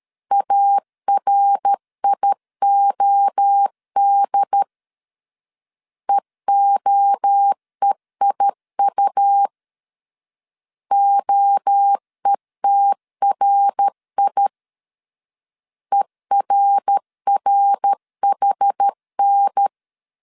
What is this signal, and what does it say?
Morse code, heard as a keyed tone, about 13 wpm, 800 Hz. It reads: ARIOD EOEIU OETRI ERRHN
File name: CW / DET / 1 3 1500C S C 12wpm 2750